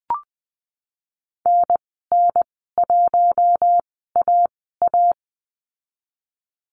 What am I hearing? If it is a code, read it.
Morse code, heard as a keyed tone, about 19 wpm, spaced as for 14 wpm, 700 Hz. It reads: NN1AA
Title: 055 Lesson 10g - Letter r - Call Signs 20wpm